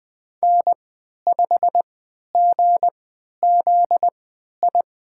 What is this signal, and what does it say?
Morse code, heard as a keyed tone, about 20 wpm, spaced as for 11 wpm, 700 Hz. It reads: N5GZI
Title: USA Call Signs - 5 Characters - 3x Character Spacing - 20wpm